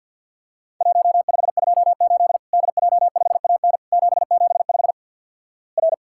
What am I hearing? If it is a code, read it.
Morse code, heard as a keyed tone, about 50 wpm, 700 Hz. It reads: 1528 DJ5AN 775 R
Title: Worked All Europe DX Contest - QTC Exchange with Repeats - 50wpm